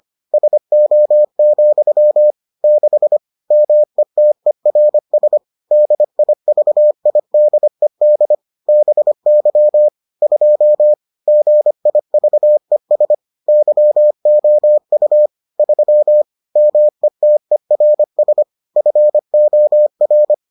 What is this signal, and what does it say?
Morse code, heard as a keyed tone, about 25 wpm, 600 Hz. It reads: SO, 6 METERS DIVIDED BY 2 GIVES YOU 3 METERS FOR